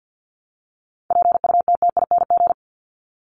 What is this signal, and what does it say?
Morse code, heard as a keyed tone, about 53 wpm, 700 Hz. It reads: PUTTING
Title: ING Suffix - Top 100 Words with Base Words 50wpm